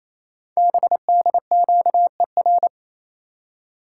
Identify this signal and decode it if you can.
Morse code, heard as a keyed tone, about 27 wpm, 700 Hz. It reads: BDQER
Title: Code Groups - 5 Letters - 28wpm